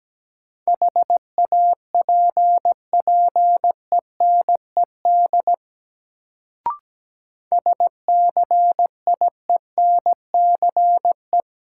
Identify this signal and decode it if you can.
Morse code, heard as a keyed tone, about 17 wpm, 700 Hz. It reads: HAPPENED SCIENCE
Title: Sentences from Top 1000 Words - Review Part 2 - New Word Drill - 17wpm